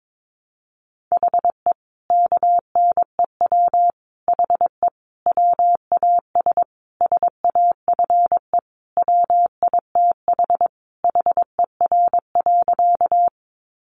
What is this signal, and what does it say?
Morse code, heard as a keyed tone, about 22 wpm, 700 Hz. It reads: HE KNEW HE WAS SAFE WITH HER.
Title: Sentences from Top 900 Words 22wpm